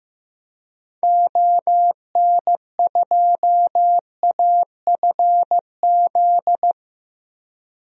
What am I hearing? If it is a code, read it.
Morse code, heard as a keyed tone, about 16 wpm, 700 Hz. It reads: ON2AFZ
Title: DXCC-100 Call Signs - 6 Characters - 15wpm